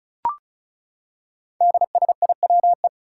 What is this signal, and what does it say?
Morse code, heard as a keyed tone, about 36 wpm, 700 Hz. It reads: DSIWE
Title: Five Challenging Letter-Number Combinations 35wpm